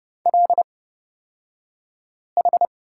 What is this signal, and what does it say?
Morse code, heard as a keyed tone, about 31 wpm, spaced as for 5 wpm, 700 Hz. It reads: LH
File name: Lesson 38d - Letter q - Two Characters 30wpm